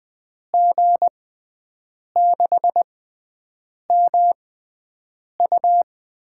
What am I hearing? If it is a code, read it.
Morse code, heard as a keyed tone, about 20 wpm, spaced as for 7 wpm, 700 Hz. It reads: G6MU
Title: USA Call Signs - 5 Characters - 6x Character Spacing - 20wpm